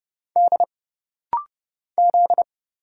Morse code, heard as a keyed tone, about 31 wpm, 700 Hz. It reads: D Z
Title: Single Letters - Rapid-Fire 30wpm